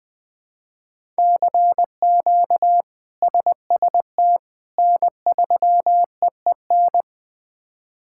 Morse code, heard as a keyed tone, about 20 wpm, 700 Hz. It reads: CQ SST N3EEN
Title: K1USN SST QSO Practice - 20wpm